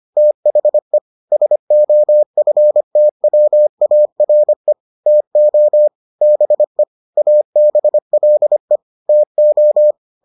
Morse code, heard as a keyed tone, about 25 wpm, 600 Hz. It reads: THE SOFTWARE TO BE ABLE TO